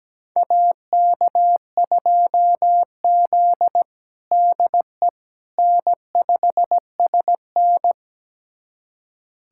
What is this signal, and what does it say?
Morse code, heard as a keyed tone, about 17 wpm, 700 Hz. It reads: AK2Z DE N5SN
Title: USA Callsign DE Callsign - 4 Characters - 17wpm